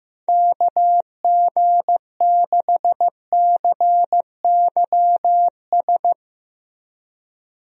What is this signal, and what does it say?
Morse code, heard as a keyed tone, about 15 wpm, 700 Hz. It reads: KG6CYS
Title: USA Call Signs - 6 Characters - Repeated 2x - 15wpm